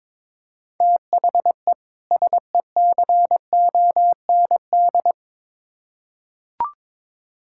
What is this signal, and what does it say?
Morse code, heard as a keyed tone, about 22 wpm, 700 Hz. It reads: THE SECOND
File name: Build A Phrase - 2-Word Phrases - 22wpm